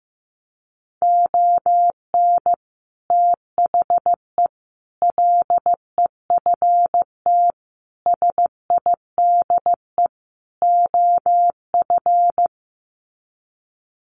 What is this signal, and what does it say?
Morse code, heard as a keyed tone, about 15 wpm, 700 Hz. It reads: ON THE LEFT SIDE OF
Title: Sets of 5 Words Encore2 15wpm